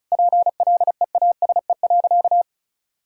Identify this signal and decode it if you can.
Morse code, heard as a keyed tone, about 36 wpm, 700 Hz. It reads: PLEASE.
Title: Sentences from Top 1600 Words 35wpm